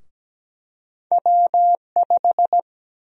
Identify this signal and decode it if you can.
Morse code, heard as a keyed tone, about 17 wpm, 700 Hz. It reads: W5
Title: USA Call Sign Prefixes - 17wpm